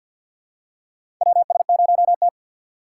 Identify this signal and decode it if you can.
Morse code, heard as a keyed tone, about 50 wpm, 700 Hz. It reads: WS0T
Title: USA Call Signs - 4 Characters - 6x Character Spacing - 50wpm